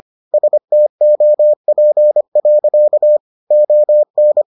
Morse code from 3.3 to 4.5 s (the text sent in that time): ON